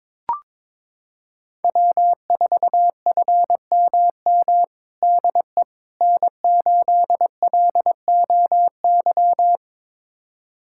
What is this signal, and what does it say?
Morse code, heard as a keyed tone, about 21 wpm, 700 Hz. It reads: W4FMM DE N8LOY
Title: USA Callsign DE Callsign - 5 Characters - 22wpm